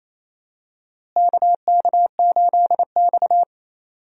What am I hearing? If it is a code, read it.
Morse code, heard as a keyed tone, about 28 wpm, 700 Hz. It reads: KK8X
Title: USA Call Signs - 4 Characters - Repeated 2x - 28wpm